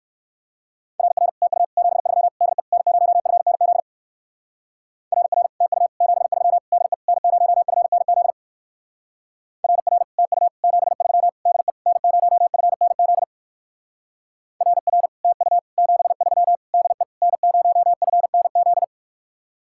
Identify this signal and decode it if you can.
Morse code, heard as a keyed tone, about 63 wpm, 700 Hz. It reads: RR TU 73 DE N0FNZ RR TU 73 DE N0FNZ RR TU 73 DE N0FNZ RR TU 73 DE N0FNZ